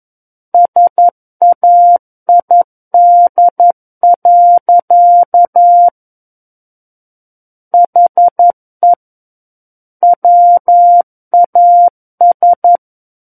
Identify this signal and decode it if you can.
Morse code, heard as a keyed tone, about 11 wpm, 700 Hz. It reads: SAID. HE WAS
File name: Book One / 02-Chapter 2 - 11 WPM